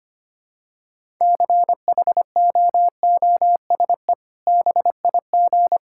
Morse code, heard as a keyed tone, about 25 wpm, 700 Hz. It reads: CHOOSE BIG
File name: practice / 25wpm / Sentences with 2-Word Phrases - 25wpm